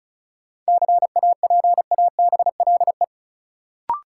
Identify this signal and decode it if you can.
Morse code, heard as a keyed tone, about 36 wpm, 700 Hz. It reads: CAPABLE
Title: Sentences from Top 1200 Words - New Word Drill Alt - 35wpm